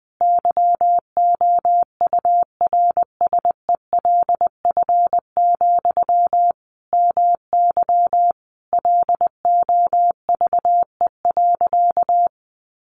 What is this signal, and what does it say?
Morse code, heard as a keyed tone, about 20 wpm, 700 Hz. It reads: YOURSELF, MY LOVE.
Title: Sentences from Top 1100 Words 20wpm